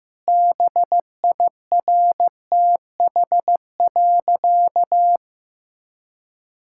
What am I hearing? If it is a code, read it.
Morse code, heard as a keyed tone, about 15 wpm, 700 Hz. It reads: BIRTH.
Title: Sentences from Top 1000 Words - Review Part 5 - 15wpm